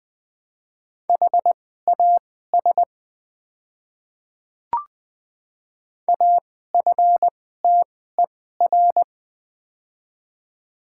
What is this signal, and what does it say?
Morse code, heard as a keyed tone, about 21 wpm, spaced as for 15 wpm, 700 Hz. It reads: HAS AFTER
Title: 2x Character Spacing - Top 200 Words - 20wpm